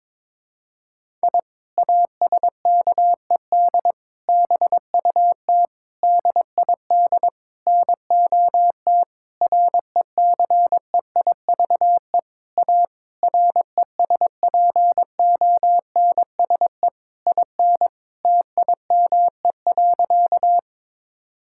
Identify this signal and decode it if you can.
Morse code, heard as a keyed tone, about 22 wpm, 700 Hz. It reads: I ASKED BUT DID NOT RECEIVE A RESPONSE IN TIME.
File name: practice / 22wpm / Sentences with 8-Word Phrases - 22wpm